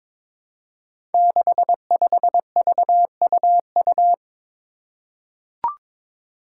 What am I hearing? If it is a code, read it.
Morse code, heard as a keyed tone, about 22 wpm, 700 Hz. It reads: 65VUU
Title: Five Challenging Letter-Number Combinations 22wpm